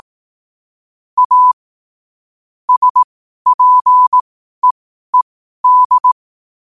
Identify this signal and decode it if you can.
Morse code, heard as a keyed tone, about 18 wpm, spaced as for 12 wpm, 1000 Hz. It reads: A SPEED
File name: qotdfiles / 12wpm